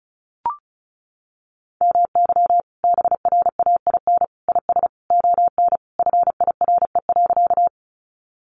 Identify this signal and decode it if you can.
Morse code, heard as a keyed tone, about 34 wpm, 700 Hz. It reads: MY BRAIN IS ON FIRE.
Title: Sentences from Top 1100 Words 35wpm